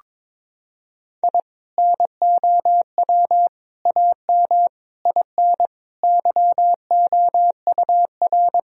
Morse code, heard as a keyed tone, about 22 wpm, 700 Hz. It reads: I NOW AM IN YOUR
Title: Sentences from Top 1000 Words - Review Part 2 - 22wpm